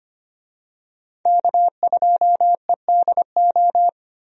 Morse code, heard as a keyed tone, about 25 wpm, 700 Hz. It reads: K2EDO
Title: USA Call Signs - 5 Characters - Repeated 3x - 25wpm